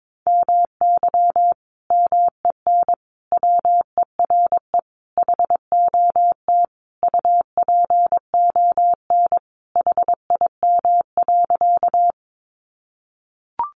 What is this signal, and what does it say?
Morse code, heard as a keyed tone, about 22 wpm, 700 Hz. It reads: MY MEN WERE HOT UPON HIM.
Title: Sentences from Top 1100 Words 22wpm